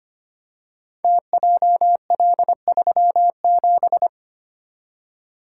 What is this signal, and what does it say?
Morse code, heard as a keyed tone, about 25 wpm, 700 Hz. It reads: TJL37
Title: Code Groups - 5 Letter:Numbers - 25wpm